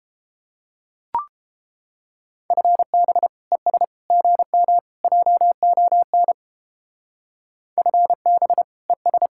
Morse code, heard as a keyed tone, about 29 wpm, 700 Hz. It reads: FB ES GM JON FB ES